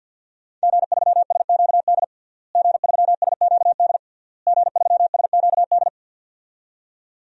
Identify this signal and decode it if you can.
Morse code, heard as a keyed tone, about 51 wpm, 700 Hz. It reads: G3SQD G3SQD G3SQD
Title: DXCC-100 Call Signs - 5 Characters - Repeated 3x - 50wpm